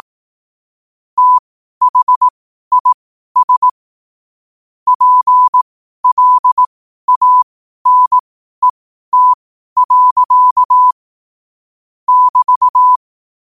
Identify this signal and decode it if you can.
Morse code, heard as a keyed tone, about 18 wpm, spaced as for 12 wpm, 1000 Hz. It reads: THIS PLANET. <BT>
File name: qotdfiles / 12wpm